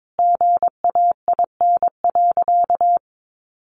Morse code, heard as a keyed tone, about 22 wpm, 700 Hz. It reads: GAIN.
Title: Sentences from Top 1100 Words 22wpm